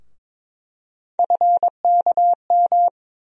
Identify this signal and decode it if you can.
Morse code, heard as a keyed tone, about 21 wpm, 700 Hz. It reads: FKM
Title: USA Call Sign Postfixes - 3 Letters 22wpm